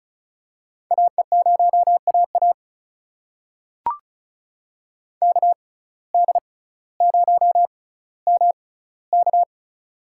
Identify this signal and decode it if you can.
Morse code, heard as a keyed tone, about 34 wpm, spaced as for 18 wpm, 700 Hz. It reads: AE0AA KD0MK
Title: USA Call Signs - 5 Characters - 6x Character Spacing - 35wpm